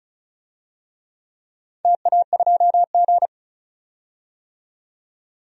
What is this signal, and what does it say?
Morse code, heard as a keyed tone, about 35 wpm, 700 Hz. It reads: TA2G
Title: DXCC-100 Call Signs - 4 Characters - 35wpm